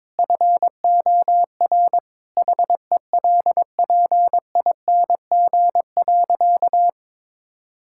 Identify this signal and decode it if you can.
Morse code, heard as a keyed tone, about 22 wpm, 700 Hz. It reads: FOR HELPING.
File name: practice / 22wpm / Sentences with 3-Word Phrases - 22wpm